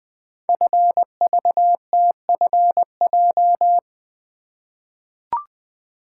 Morse code, heard as a keyed tone, about 20 wpm, 700 Hz. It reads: FVTFJ
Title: Code Groups - 5 Letters - 20wpm